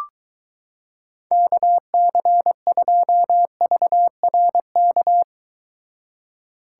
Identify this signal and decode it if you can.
Morse code, heard as a keyed tone, about 23 wpm, 700 Hz. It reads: KC2VRK